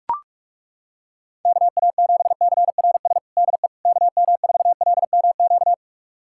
Morse code, heard as a keyed tone, about 41 wpm, 700 Hz. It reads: KA7CRS DE KM4LMQ
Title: USA Callsign DE Callsign 45wpm